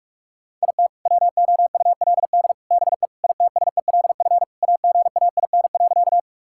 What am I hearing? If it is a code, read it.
Morse code, heard as a keyed tone, about 44 wpm, 700 Hz. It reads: IT WOULD BE ITSELF AGAIN.
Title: Sentences from Top 1000 Words - Review Part 5 - 45wpm